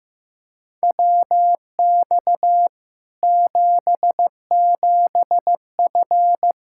Morse code, heard as a keyed tone, about 16 wpm, 700 Hz. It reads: WX 77F
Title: Practice QSO - Exchange 2 - Info - 15wpm